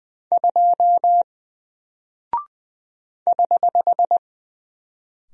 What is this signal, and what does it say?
Morse code, heard as a keyed tone, about 20 wpm, 700 Hz. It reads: 2 <HH>
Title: Single Character (Letter:Number:Punctuation:Prosign) Alt 20wpm